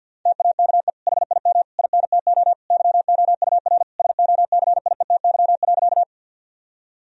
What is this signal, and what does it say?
Morse code, heard as a keyed tone, about 50 wpm, 700 Hz. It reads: TAKE HIM INTO YOUR SOCIETY.
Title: Sentences from Top 1000 Words - Review Part 4 - 50wpm